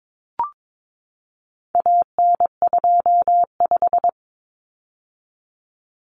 Morse code, heard as a keyed tone, about 22 wpm, 700 Hz. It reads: AN25
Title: Common Cut Numbers - 4-Digit 22wpm